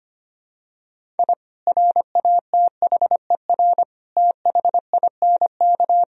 Morse code, heard as a keyed tone, about 25 wpm, 700 Hz. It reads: I RATHER THINK
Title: Sentences from Top 1000 Words - Review Part 4 - 25wpm